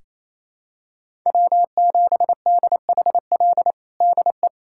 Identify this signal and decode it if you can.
Morse code, heard as a keyed tone, about 27 wpm, 700 Hz. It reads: W7DHL DE
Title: USA Callsign DE Callsign - 5 Characters - 28wpm